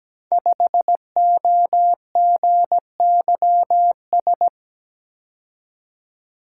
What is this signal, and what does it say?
Morse code, heard as a keyed tone, about 17 wpm, 700 Hz. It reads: 5OGYS
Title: Code Groups - 5 Letter:Numbers - 17wpm